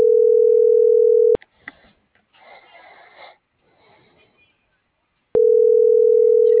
Ringback tone (North America), 440 + 480 Hz.